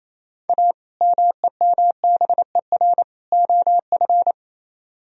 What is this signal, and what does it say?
Morse code, heard as a keyed tone, about 34 wpm, 700 Hz. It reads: A MEMBER OF
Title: Speed-Racing - Sets of 3 Words 28wpm